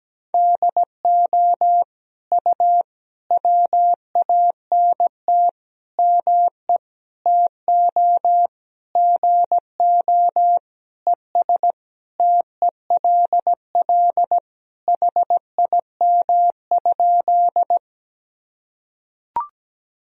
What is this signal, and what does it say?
Morse code, heard as a keyed tone, about 17 wpm, 700 Hz. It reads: DO U WANT ME TO GO ES TELL HIM?
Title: Sentences from Top 200 Words and Abbreviations 17wpm